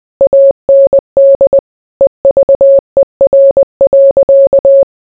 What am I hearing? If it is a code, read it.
Morse code, heard as a keyed tone, about 20 wpm, 550 Hz. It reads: AND EVER.